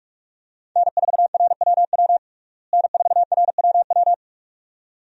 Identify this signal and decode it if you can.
Morse code, heard as a keyed tone, about 46 wpm, 700 Hz. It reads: N4RWW N4RWW
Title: USA Call Signs - 5 Characters - Repeated 2x - 45wpm